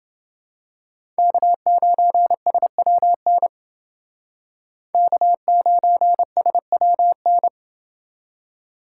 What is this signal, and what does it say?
Morse code, heard as a keyed tone, about 29 wpm, 700 Hz. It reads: K9SWN K9SWN